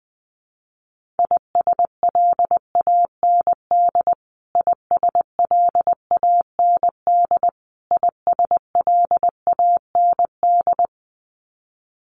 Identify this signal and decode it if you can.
Morse code, heard as a keyed tone, about 20 wpm, 700 Hz. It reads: ISLAND ISLAND ISLAND
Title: Top 500 Words Repeated 20wpm